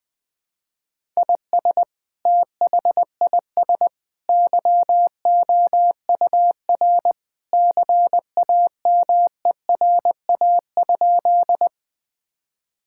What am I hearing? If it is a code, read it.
Morse code, heard as a keyed tone, about 20 wpm, 700 Hz. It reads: IS THIS YOUR CAMERA?
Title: Sentences from Top 1000 Words - Review Part 5 - 20wpm